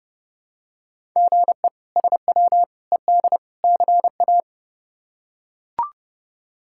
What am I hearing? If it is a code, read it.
Morse code, heard as a keyed tone, about 31 wpm, 700 Hz. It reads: GE SW ED CA
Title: K1USN SST QSO Practice - 30wpm